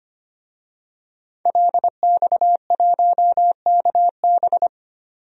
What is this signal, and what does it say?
Morse code, heard as a keyed tone, about 25 wpm, 700 Hz. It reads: LX1KB
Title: DXCC-100 Call Signs - 5 Characters - 25wpm